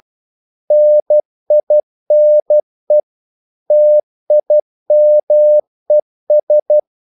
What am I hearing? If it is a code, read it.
Morse code, heard as a keyed tone, about 12 wpm, 600 Hz. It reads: NINE TIMES
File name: CW / Practice / EnglishQuotations0312